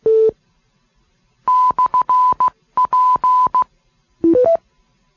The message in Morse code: /P